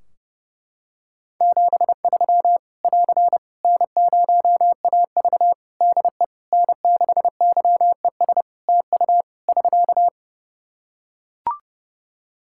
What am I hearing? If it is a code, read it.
Morse code, heard as a keyed tone, about 30 wpm, 700 Hz. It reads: 73 <AR> N0AV DE N6YES TU <SK>
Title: Practice QSO - Exchange 4 - Ending - 30wpm